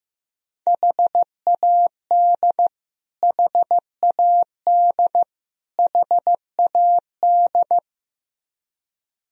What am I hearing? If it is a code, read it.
Morse code, heard as a keyed tone, about 15 wpm, 700 Hz. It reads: HAD HAD HAD
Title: Top 100 Words Repeated 15wpm